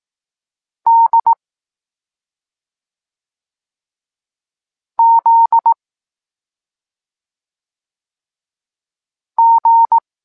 Morse code, heard as a keyed tone, about 18 wpm, 920 Hz. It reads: D Z G
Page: Tekst 23.3 – Ciąg dalszy doskonalenia odbioru poznanych liter